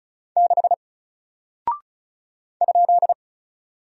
Morse code, heard as a keyed tone, about 36 wpm, 700 Hz. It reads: 6 ?
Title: Single Character (Letter:Number:Punctuation:Prosign) Alt 35wpm